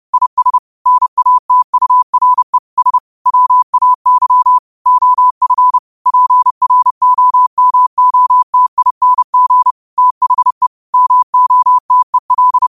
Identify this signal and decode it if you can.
Morse code, heard as a keyed tone, about 31 wpm, 1000 Hz. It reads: IS NATURE S WAY OF PROMOTING THE MOTEL
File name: qotdfiles / 30wpm